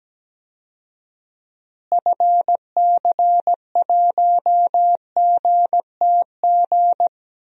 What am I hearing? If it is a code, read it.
Morse code, heard as a keyed tone, about 17 wpm, 700 Hz. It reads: FC1GTG
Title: DXCC-100 Call Signs - 6 Characters - Repeated 3x - 17wpm